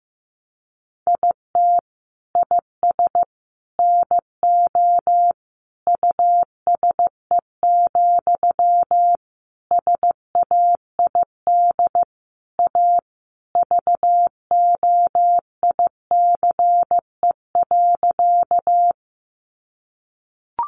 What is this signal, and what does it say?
Morse code, heard as a keyed tone, about 15 wpm, 700 Hz. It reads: IT IS NO USE, SAID A VOICE.
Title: Sentences from Top 500 Words 15wpm